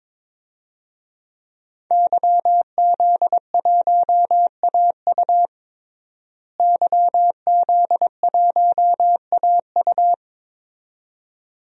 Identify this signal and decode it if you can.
Morse code, heard as a keyed tone, about 22 wpm, 700 Hz. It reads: YZ1AU YZ1AU
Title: DXCC-100 Call Signs - 5 Characters - Repeated 2x - 22wpm